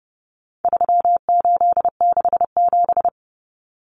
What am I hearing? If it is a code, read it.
Morse code, heard as a keyed tone, about 30 wpm, 700 Hz. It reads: 3867
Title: Numbers - 4-Digit 30wpm